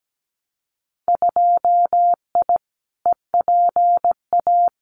Morse code, heard as a keyed tone, about 17 wpm, 700 Hz. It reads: 2I EPA
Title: Winter Field Day Exchanges 17wpm